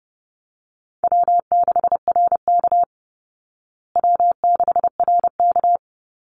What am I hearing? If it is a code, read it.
Morse code, heard as a keyed tone, about 30 wpm, 700 Hz. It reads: W6RK W6RK
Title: Call Signs Repeated 2x - 30wpm